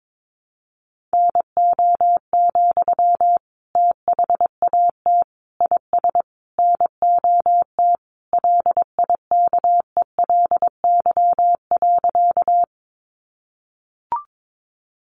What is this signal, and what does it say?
Morse code, heard as a keyed tone, about 22 wpm, 700 Hz. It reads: NO, THAT IS NOT LIKELY.
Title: Sentences from Top 600 Words 22wpm